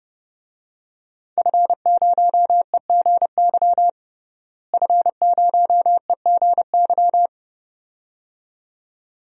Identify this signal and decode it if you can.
Morse code, heard as a keyed tone, about 30 wpm, 700 Hz. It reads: F0EGY F0EGY
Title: DXCC-100 Call Signs - 5 Characters - Repeated 2x - 30wpm